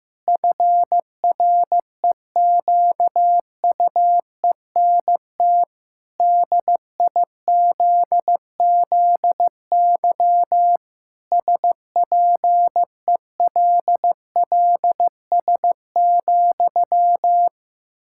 Morse code, heard as a keyed tone, about 15 wpm, 700 Hz. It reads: FREQUENT DIZZY SPELLS,
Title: Pangram Challenge v2 - 15wpm